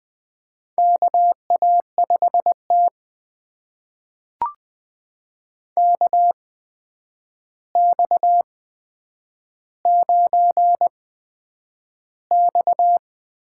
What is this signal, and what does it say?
Morse code, heard as a keyed tone, about 20 wpm, spaced as for 9 wpm, 700 Hz. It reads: KA5T KX9X
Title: USA Call Signs - 4 Characters - 8x Character Spacing - 20wpm